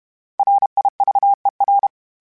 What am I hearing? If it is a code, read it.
Morse code, heard as a keyed tone, about 32 wpm, 800 Hz. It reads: RIVER
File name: fromSMs / FromSMs mp3s / Cities@30